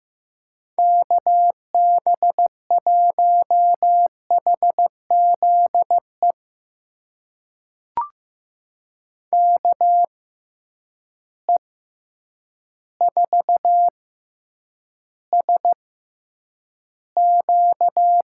Morse code, heard as a keyed tone, about 15 wpm, spaced as for 8 wpm, 700 Hz. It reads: KB1HZE KE4SQ
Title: USA Call Signs - 6 Characters - 6x Character Spacing - 15wpm